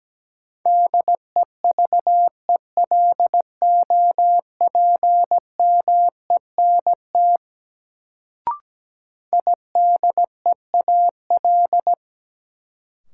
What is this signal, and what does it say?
Morse code, heard as a keyed tone, about 17 wpm, 700 Hz. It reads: DEVELOPMENT IDEAL
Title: Sentences from Top 1400 Words - New Word Drill Alt - 17wpm